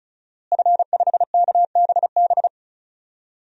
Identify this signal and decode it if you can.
Morse code, heard as a keyed tone, about 36 wpm, 700 Hz. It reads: F5KBB
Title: DXCC-100 Call Signs - 5 Characters - Repeated 3x - 35wpm